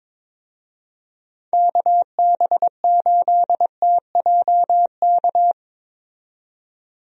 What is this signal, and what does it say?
Morse code, heard as a keyed tone, about 22 wpm, 700 Hz. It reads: KB8TJK
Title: USA Call Signs - 6 Characters - 22wpm